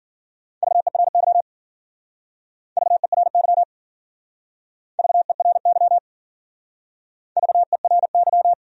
Morse code, heard as a keyed tone, about 52 wpm, 700 Hz. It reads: VERY VERY VERY VERY